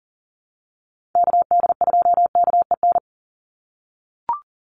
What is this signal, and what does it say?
Morse code, heard as a keyed tone, about 40 wpm, 700 Hz. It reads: KD2KEN
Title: USA Call Signs 40wpm